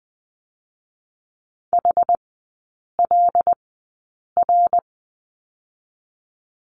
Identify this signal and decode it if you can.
Morse code, heard as a keyed tone, about 19 wpm, 700 Hz. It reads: H L R
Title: 114 Lesson 19e - Letter w - Three Characters 20wpm